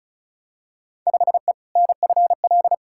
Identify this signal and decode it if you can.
Morse code, heard as a keyed tone, about 36 wpm, 700 Hz. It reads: 5E NFL
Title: Field Day Exchanges Encore - 35wpm